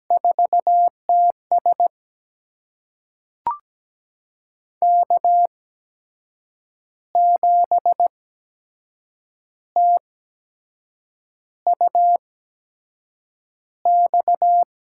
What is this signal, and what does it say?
Morse code, heard as a keyed tone, about 17 wpm, spaced as for 6 wpm, 700 Hz. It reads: 4TS K7TUX
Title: USA Call Signs - 5 Characters - 8x Character Spacing - 17wpm